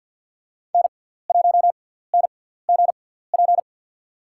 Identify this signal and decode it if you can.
Morse code, heard as a keyed tone, about 53 wpm, spaced as for 17 wpm, 700 Hz. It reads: N1NGP
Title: USA Call Signs - 5 Characters - 6x Character Spacing - 50wpm